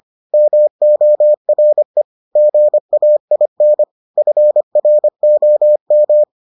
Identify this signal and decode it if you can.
Morse code, heard as a keyed tone, about 25 wpm, 600 Hz. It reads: MORE GAIN FROM